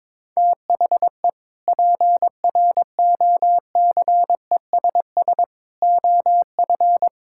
Morse code, heard as a keyed tone, about 22 wpm, 700 Hz. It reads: THE PROCESS OF
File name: practice / 22wpm / Sentences with 3-Word Phrases - 22wpm